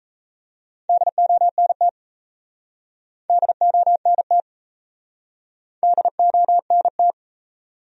Morse code, heard as a keyed tone, about 39 wpm, 700 Hz. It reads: DONT DONT DONT